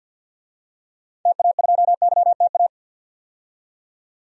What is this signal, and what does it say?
Morse code, heard as a keyed tone, about 52 wpm, 700 Hz. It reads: TA2YTA
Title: DXCC-100 Call Signs - 6 Characters - 50wpm